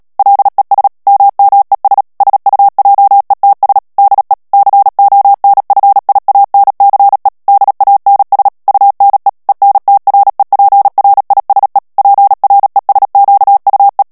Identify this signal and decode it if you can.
Morse code, heard as a keyed tone, about 37 wpm, 800 Hz. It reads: LES MMES SUJETS DE CONFIANCE DANS UNE ENTREPRISE PRESQUE
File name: CW / FICHIERS-MP3 / FRANCAIS TEXTE CLAIR 2 VITESSE 35 MOTS MINUTES